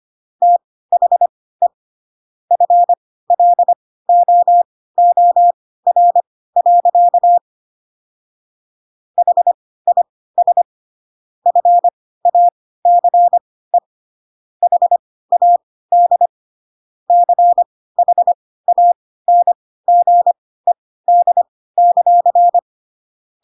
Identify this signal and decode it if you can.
Morse code, heard as a keyed tone, about 25 wpm, spaced as for 16 wpm, 700 Hz. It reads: THE FLOOR. HIS FACE HAD CHANGED;
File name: Pinocchio / Pinocchio ch1 10 25 ik1hge